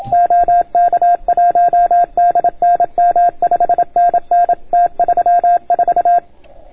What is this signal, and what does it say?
Morse code, heard as a keyed tone, about 29 wpm, 710 Hz. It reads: OK1DNM 5NN T34